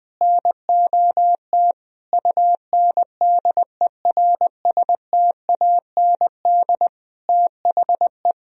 Morse code, heard as a keyed tone, about 20 wpm, 700 Hz. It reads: NOT UNDERSTAND THE
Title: Sentences from Top 1000 Words - Review Part 3 - 20wpm